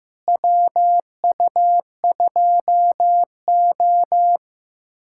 Morse code, heard as a keyed tone, about 15 wpm, 700 Hz. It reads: WU2O
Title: USA Callsign DE Callsign - 4 Characters - 15wpm